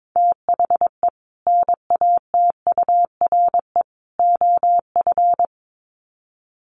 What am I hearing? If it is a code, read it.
Morse code, heard as a keyed tone, about 22 wpm, 700 Hz. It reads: THE NATURE OF
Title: Sets of 3 Words 22wpm